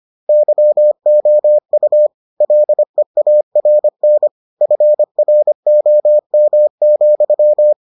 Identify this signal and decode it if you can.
Morse code, heard as a keyed tone, about 25 wpm, 600 Hz. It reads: YOU LEARN FROM,